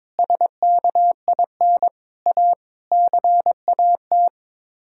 Morse code, heard as a keyed tone, about 22 wpm, 700 Hz. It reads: SKIN A CAT
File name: Sets of 9 Words - 22wpm